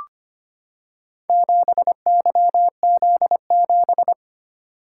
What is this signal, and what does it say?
Morse code, heard as a keyed tone, about 25 wpm, 700 Hz. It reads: 7YZ7
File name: practice / 25wpm / Four Letter-Number Combinations - 25wpm